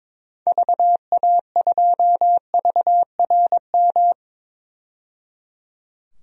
Morse code, heard as a keyed tone, about 22 wpm, 700 Hz. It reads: VA2VRM
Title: DXCC-100 Call Signs - 6 Characters - 22wpm